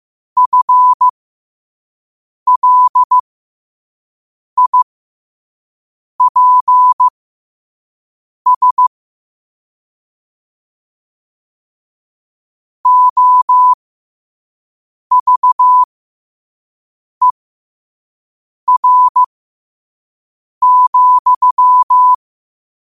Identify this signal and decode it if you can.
Morse code, heard as a keyed tone, about 15 wpm, spaced as for 5 wpm, 1000 Hz. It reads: FLIPS OVER,
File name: qotdfiles / 5wpm